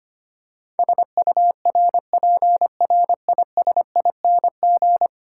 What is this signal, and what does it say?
Morse code, heard as a keyed tone, about 25 wpm, 700 Hz. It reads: SURPRISING
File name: 10 Letter Words - First 3 Letters Given - 25wpm